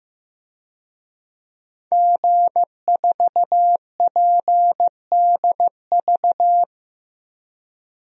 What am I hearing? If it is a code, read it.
Morse code, heard as a keyed tone, about 15 wpm, 700 Hz. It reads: G4PDV
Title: DXCC-100 Call Signs - 5 Characters - 15wpm